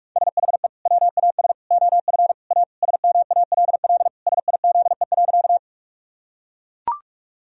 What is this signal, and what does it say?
Morse code, heard as a keyed tone, about 41 wpm, 700 Hz. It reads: SHE WAS OF A SMALL SIZE.
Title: Sentences from Top 1000 Words - Review Part 2 - 45wpm